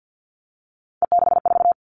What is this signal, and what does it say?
Morse code, heard as a keyed tone, about 50 wpm, 700 Hz. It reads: E64
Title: Three Challenging Letter-Number Combinations 50wpm